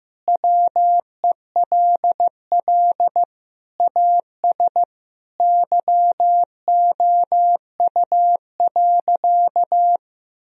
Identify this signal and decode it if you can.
Morse code, heard as a keyed tone, about 15 wpm, 700 Hz. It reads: WELL AS YOU.
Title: Sentences from Top 1000 Words - Review Part 3 - 15wpm